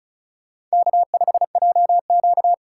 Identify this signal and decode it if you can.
Morse code, heard as a keyed tone, about 36 wpm, 700 Hz. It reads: K5JQ
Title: USA Call Signs - 4 Characters - 3x Character Spacing - 35wpm